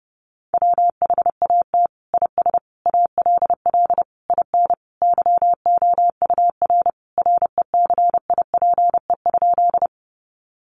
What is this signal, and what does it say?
Morse code, heard as a keyed tone, about 30 wpm, 700 Hz. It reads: WHAT IS ALL IN YOUR RECIPE?
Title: Sentences from Top 1200 Words 30wpm